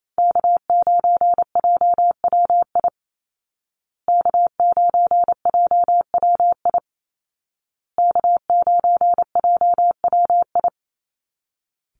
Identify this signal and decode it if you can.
Morse code, heard as a keyed tone, about 28 wpm, 700 Hz. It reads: K9JWI K9JWI K9JWI
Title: Call Signs Repeated 2x - 28wpm